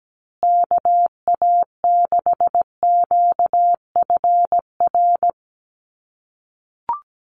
Morse code, heard as a keyed tone, about 17 wpm, 700 Hz. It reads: KA6QFR
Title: USA Call Signs Repeated 3x 17wpm